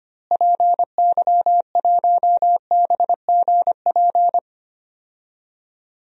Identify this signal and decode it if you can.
Morse code, heard as a keyed tone, about 25 wpm, 700 Hz. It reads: PY1BGP
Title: DXCC-100 Call Signs - 6 Characters - Repeated 3x - 25wpm